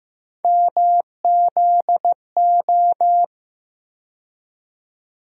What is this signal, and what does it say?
Morse code, heard as a keyed tone, about 15 wpm, 700 Hz. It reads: MZO
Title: USA Call Sign Postfixes - 3 Letters 15wpm